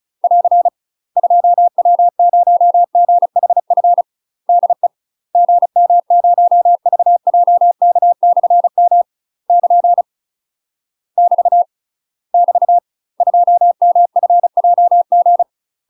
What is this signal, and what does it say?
Morse code, heard as a keyed tone, about 36 wpm, 700 Hz. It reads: <AR> 2W0GHF DE GM0VJK/M <KN> <BT> <BT> 2MFJG
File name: UK MASTER QSOs TRAINING / 35WPM / UK MASTER QSOs 0011